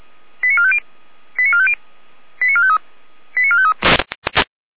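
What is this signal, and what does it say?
Five-tone selective call (ZVEI), heard as quick runs of five tones, tones of about 70 ms. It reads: 89450 89450 89453 89453
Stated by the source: ZVEI five-tone selective call, code 88177, received by SDR.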